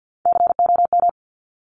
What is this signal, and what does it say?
Morse code, heard as a keyed tone, about 50 wpm, 700 Hz. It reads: COM
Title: Top Three Letter Combinations in Words 50wpm